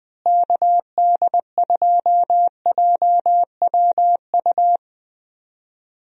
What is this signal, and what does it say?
Morse code, heard as a keyed tone, about 20 wpm, 700 Hz. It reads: KD2JWU
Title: USA Call Signs - 6 Characters - Repeated 3x - 20wpm